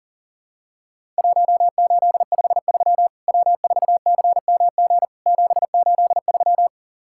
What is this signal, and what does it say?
Morse code, heard as a keyed tone, about 40 wpm, 700 Hz. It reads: 1853 W4CMG 783
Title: Worked All Europe DX Contest - QTC Exchange - 40wpm